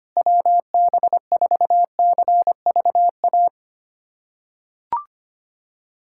Morse code, heard as a keyed tone, about 25 wpm, 700 Hz. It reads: WB4CVA
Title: USA Callsign DE Callsign 25wpm